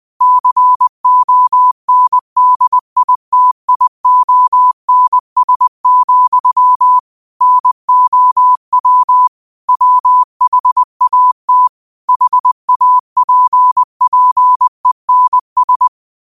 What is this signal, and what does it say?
Morse code, heard as a keyed tone, about 20 wpm, 1000 Hz. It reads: CONDITIONS, NOW WHAT HAPPENS